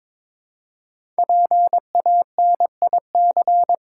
Morse code, heard as a keyed tone, about 23 wpm, 700 Hz. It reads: PANIC
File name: 5 Letter Words - First 3 Letters Given - 22wpm